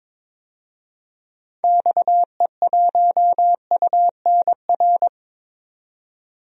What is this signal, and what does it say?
Morse code, heard as a keyed tone, about 22 wpm, 700 Hz. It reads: XE1UNR
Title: DXCC-100 Call Signs - 6 Characters - 22wpm